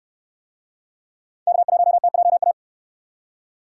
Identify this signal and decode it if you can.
Morse code, heard as a keyed tone, about 69 wpm, 700 Hz. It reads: K1TJA